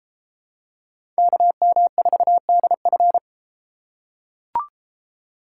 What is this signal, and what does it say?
Morse code, heard as a keyed tone, about 31 wpm, 700 Hz. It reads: KM4DF